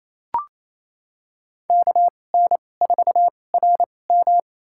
Morse code, heard as a keyed tone, about 28 wpm, spaced as for 20 wpm, 700 Hz. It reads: KN4RM
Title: USA Call Signs - 5 Characters - 2x Character Spacing - 28wpm